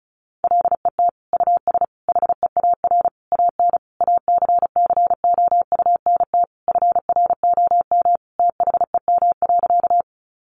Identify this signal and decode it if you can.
Morse code, heard as a keyed tone, about 34 wpm, 700 Hz. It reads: LET US HEAR AN ACCOUNT FROM THEM.
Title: Sentences from Top 900 Words 35wpm